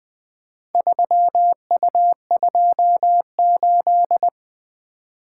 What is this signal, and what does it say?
Morse code, heard as a keyed tone, about 20 wpm, 700 Hz. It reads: 3U28
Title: Four Letter-Number Combinations - 20wpm